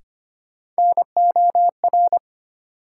Morse code, heard as a keyed tone, about 25 wpm, 700 Hz. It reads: NOR
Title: Sentences from Top 1500 Words - New Word Drill - 25wpm